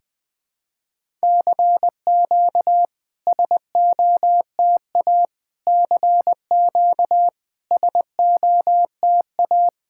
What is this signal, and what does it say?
Morse code, heard as a keyed tone, about 20 wpm, 700 Hz. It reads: CQ SOTA CQ SOTA
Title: SOTA QSO Practice 20wpm